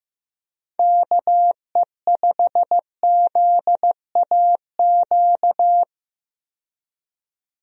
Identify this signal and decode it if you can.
Morse code, heard as a keyed tone, about 15 wpm, 700 Hz. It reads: KE5ZAQ
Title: USA Call Signs - 6 Characters - 15wpm